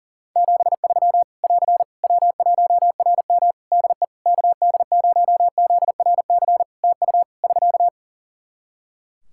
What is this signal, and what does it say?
Morse code, heard as a keyed tone, about 40 wpm, 700 Hz. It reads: 73 <AR> W1RM DE KD0ZRC TU <SK>
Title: Practice QSO - Exchange 4 - Ending - 40wpm